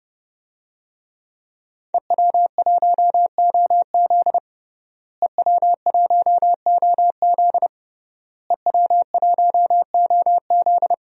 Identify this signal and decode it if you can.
Morse code, heard as a keyed tone, about 32 wpm, 700 Hz. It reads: EW1OZ EW1OZ EW1OZ
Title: DXCC-100 Call Signs - 5 Characters - Repeated 3x - 30wpm